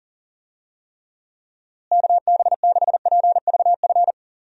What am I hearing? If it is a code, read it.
Morse code, heard as a keyed tone, about 37 wpm, 700 Hz. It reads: KB6PVF
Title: USA Call Signs - 6 Characters - Repeated 2x - 40wpm